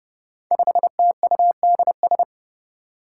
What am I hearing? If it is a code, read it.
Morse code, heard as a keyed tone, about 30 wpm, 700 Hz. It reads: 5TUDS
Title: Five Challenging Letter-Number Combinations 30wpm